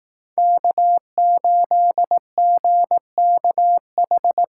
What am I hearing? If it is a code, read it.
Morse code, heard as a keyed tone, about 20 wpm, 700 Hz. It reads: K8GKH